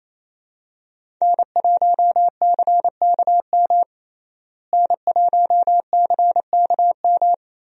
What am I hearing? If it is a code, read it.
Morse code, heard as a keyed tone, about 27 wpm, 700 Hz. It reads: N1CKM N1CKM
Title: USA Call Signs - 5 Characters - Repeated 3x - 28wpm